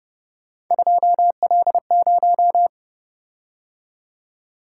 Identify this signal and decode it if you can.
Morse code, heard as a keyed tone, about 29 wpm, 700 Hz. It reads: 2L0